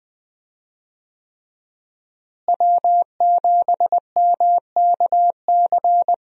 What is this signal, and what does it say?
Morse code, heard as a keyed tone, about 21 wpm, 700 Hz. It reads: W7MKC
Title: USA Call Signs - 5 Characters - Repeated 3x - 20wpm